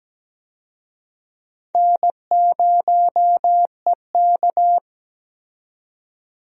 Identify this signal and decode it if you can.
Morse code, heard as a keyed tone, about 17 wpm, 700 Hz. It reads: N0EK
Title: USA Call Signs - 4 Characters - 17wpm